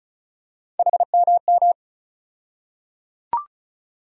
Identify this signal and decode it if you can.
Morse code, heard as a keyed tone, about 36 wpm, 700 Hz. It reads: HMM
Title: USA Call Sign Postfixes - 3 Letters 35wpm